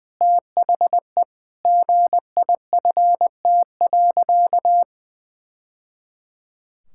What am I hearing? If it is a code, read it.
Morse code, heard as a keyed tone, about 21 wpm, 700 Hz. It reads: THE GIFT.
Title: Sentences from Top 1000 Words - Review Part 6 - 20wpm